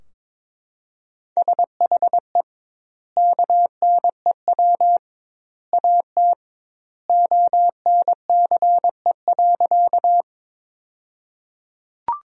Morse code, heard as a keyed tone, about 22 wpm, 700 Hz. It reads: SHE KNEW AT ONCE.
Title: 2x Word-Spacing - Sentences from Top 400 Words - 22wpm